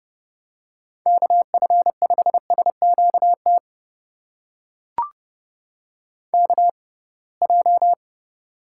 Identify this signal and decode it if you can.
Morse code, heard as a keyed tone, about 30 wpm, spaced as for 23 wpm, 700 Hz. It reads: KF5SQT KJ
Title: USA Call Signs - 6 Characters - 6x Character Spacing - 30wpm